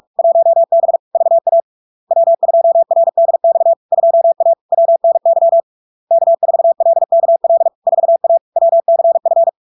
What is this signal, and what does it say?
Morse code, heard as a keyed tone, about 46 wpm, 670 Hz. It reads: 1B VA W2RDX 2A WNY K4LKL 4A WCF
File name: LARK-FD-2022-Log-in-CW-45wpm-670-Hz